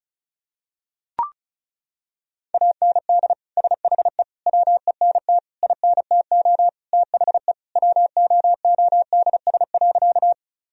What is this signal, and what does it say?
Morse code, heard as a keyed tone, about 36 wpm, 700 Hz. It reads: AND SHE WENT INTO THE WOODS.
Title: Sentences from Top 1000 Words - Review Part 2 - 35wpm